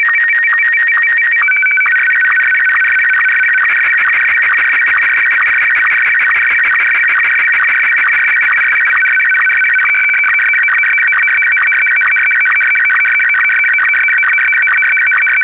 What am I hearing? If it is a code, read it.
SSTV (slow-scan television), Martin 1.
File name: digimodes / Digimodes mp3 / SSTV-Martin1